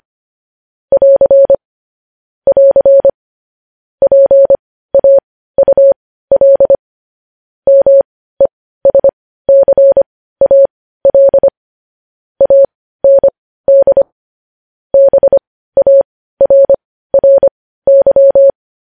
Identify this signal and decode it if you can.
Morse code, heard as a keyed tone, about 25 wpm, spaced as for 15 wpm, 550 Hz. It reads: <AR> <AR> PAUL MESCAL AND BARRY
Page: news-headlines-in-morse-code-at-15-wpm